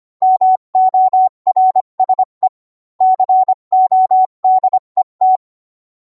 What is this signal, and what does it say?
Morse code, heard as a keyed tone, about 25 wpm, 750 Hz. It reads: MORSE CODE T